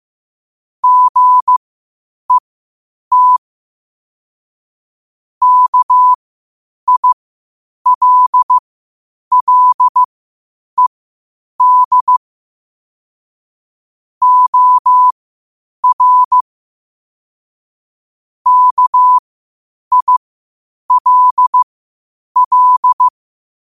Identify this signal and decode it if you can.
Morse code, heard as a keyed tone, about 15 wpm, spaced as for 8 wpm, 1000 Hz. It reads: GET KILLED OR KILL
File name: qotdfiles / 8wpm